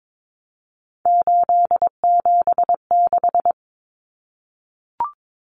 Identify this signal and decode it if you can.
Morse code, heard as a keyed tone, about 22 wpm, 700 Hz. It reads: 876
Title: Common Cut Numbers - 3-Digit 22wpm